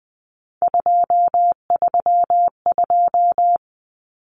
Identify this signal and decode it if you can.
Morse code, heard as a keyed tone, about 20 wpm, 700 Hz. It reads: 232
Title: Numbers - 3-Digit 20wpm